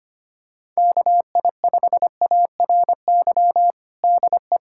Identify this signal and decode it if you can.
Morse code, heard as a keyed tone, about 25 wpm, 700 Hz. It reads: KI5ARY DE
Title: USA Callsign DE Callsign 25wpm